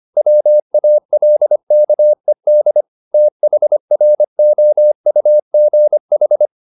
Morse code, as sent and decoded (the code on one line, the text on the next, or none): .-- .- .-.. -.- . -.. / - .... .-. --- ..- --. ....
WALKED THROUGH